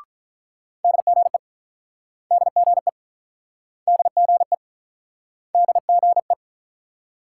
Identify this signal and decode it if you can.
Morse code, heard as a keyed tone, about 43 wpm, 700 Hz. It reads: DGE DGE DGE DGE
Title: Speed-Racing Multi-letter Phonemes - 35wpm